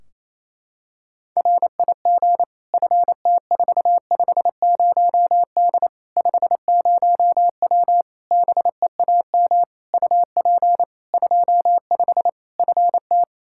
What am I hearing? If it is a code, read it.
Morse code, heard as a keyed tone, about 28 wpm, 700 Hz. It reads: RIG FT450D 50W BEAM UP 25 FT
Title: Practice QSO - Exchange 2 - Info - 28wpm